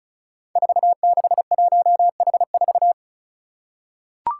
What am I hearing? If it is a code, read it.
Morse code, heard as a keyed tone, about 36 wpm, 700 Hz. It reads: 461H4
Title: Five Challenging Letter-Number Combinations 35wpm